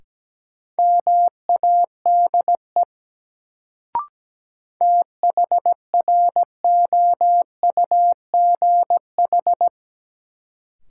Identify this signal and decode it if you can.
Morse code, heard as a keyed tone, about 17 wpm, 700 Hz. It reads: MADE THROUGH
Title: Sentences from Top 200 Words - New Word Drill - 17wpm